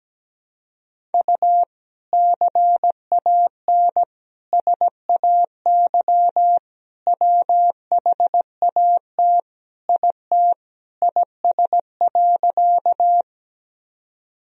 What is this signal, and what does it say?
Morse code, heard as a keyed tone, about 17 wpm, 700 Hz. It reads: U CAN SAY WHAT IT IS.
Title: Sentences from Top 200 Words and Abbreviations 17wpm